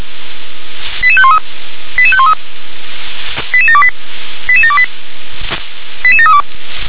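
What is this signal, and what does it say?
Five-tone selective call (ZVEI), heard as quick runs of five tones, tones of about 70 ms. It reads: 88513 88513 88728 88728 88732
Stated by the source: ZVEI five-tone selective call received on an SDR.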